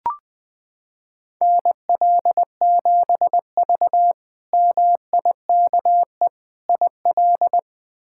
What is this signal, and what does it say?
Morse code, heard as a keyed tone, about 21 wpm, 700 Hz. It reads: NL7V MIKE IL
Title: K1USN SST QSO Practice - 20wpm